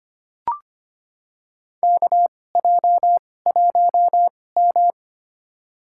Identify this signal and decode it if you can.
Morse code, heard as a keyed tone, about 25 wpm, spaced as for 18 wpm, 700 Hz. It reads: KJ1M
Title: USA Call Signs - 2x Character Spacing - 25wpm